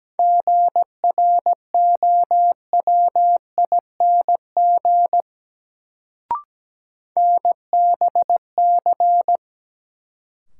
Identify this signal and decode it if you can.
Morse code, heard as a keyed tone, about 17 wpm, 700 Hz. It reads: GROWING NBC
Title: Sets of 5 Words Encore - New Word Drill - 17wpm